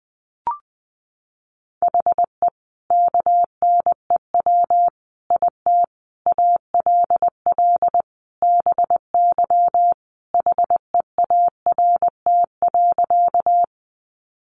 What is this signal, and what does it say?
Morse code, heard as a keyed tone, about 20 wpm, 700 Hz. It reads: HE KNEW IT ALL BY HEART.
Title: Sentences from Top 1200 Words 20wpm